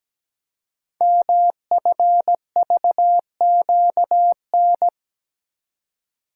Morse code, heard as a keyed tone, about 17 wpm, 700 Hz. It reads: MFVQN
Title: Code Groups - 5 Letters - 17wpm